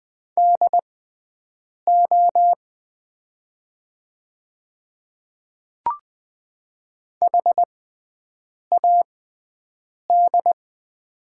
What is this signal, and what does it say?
Morse code, heard as a keyed tone, about 20 wpm, spaced as for 7 wpm, 700 Hz. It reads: DO HAD
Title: 6x Character Spacing - Top 200 Words - 20wpm